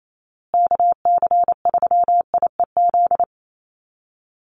Morse code, heard as a keyed tone, about 28 wpm, 700 Hz. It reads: KC3IEZ
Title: USA Call Signs Repeated 3x 28wpm